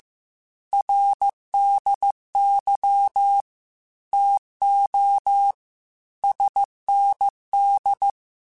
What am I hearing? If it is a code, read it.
Morse code, heard as a keyed tone, about 15 wpm, 800 Hz. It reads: RDY TO SND
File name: QSOs / QSOs-mp3s / WA7QOX 1@13